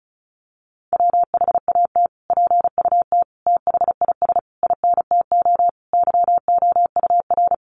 Morse code, heard as a keyed tone, about 34 wpm, 700 Hz. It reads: WHAT PUT THIS INTO YOUR
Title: Sentences from Top 600 Words 35wpm